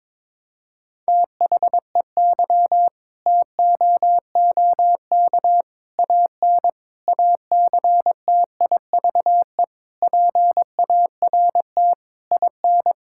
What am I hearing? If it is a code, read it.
Morse code, heard as a keyed tone, about 22 wpm, 700 Hz. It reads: THEY TOOK AN ACTIVE PART IN
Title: Sentences from Top 1000 Words - Review Part 6 - 22wpm